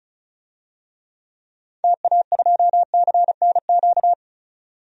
Morse code, heard as a keyed tone, about 36 wpm, 690 Hz. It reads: TA2CNQ
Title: DXCC-100 Call Signs - 6 Characters - 35wpm